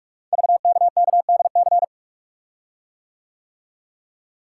Morse code, heard as a keyed tone, about 44 wpm, 690 Hz. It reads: VKKDC
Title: Code Groups - 5 Letters - 45wpm